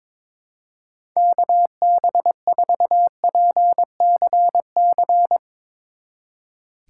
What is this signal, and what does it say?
Morse code, heard as a keyed tone, about 24 wpm, 700 Hz. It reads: KB4PCC
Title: Speed Racing - USA Call Signs 22wpm